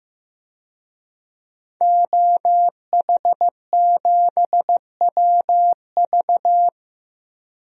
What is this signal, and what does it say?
Morse code, heard as a keyed tone, about 15 wpm, 700 Hz. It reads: OH7WV
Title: DXCC-100 Call Signs - 5 Characters - Repeated 3x - 15wpm